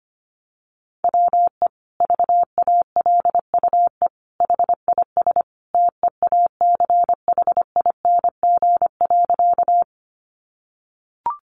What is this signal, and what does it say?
Morse code, heard as a keyed tone, about 25 wpm, 700 Hz. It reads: WE VALUE HIS TEACHING.
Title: Sentences from Top 1100 Words 25wpm